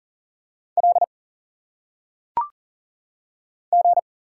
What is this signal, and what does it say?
Morse code, heard as a keyed tone, about 40 wpm, 700 Hz. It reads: L G
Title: Single Letters - Phonetic Pronunciation 40wpm